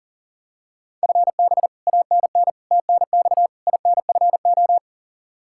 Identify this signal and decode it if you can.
Morse code, heard as a keyed tone, about 40 wpm, 700 Hz. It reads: FB ANN TNX INFO
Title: Practice QSO - Exchange 3 - More Info - 40wpm